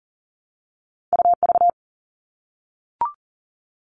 Morse code, heard as a keyed tone, about 38 wpm, 700 Hz. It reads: UV
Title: All Cut Numbers - 2-Digit 40wpm